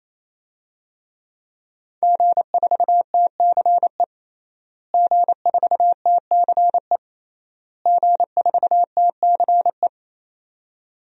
Morse code, heard as a keyed tone, about 28 wpm, 700 Hz. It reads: G4TCE G4TCE G4TCE
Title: DXCC-100 Call Signs - 5 Characters - Repeated 3x - 28wpm